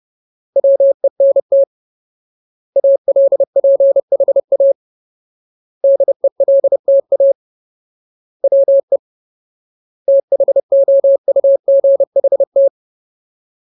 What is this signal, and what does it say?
Morse code, heard as a keyed tone, about 30 wpm, 550 Hz. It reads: WENT ALPHA DELTA WE THOUGHT
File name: Ultimate Word List / 30WPM Ultimate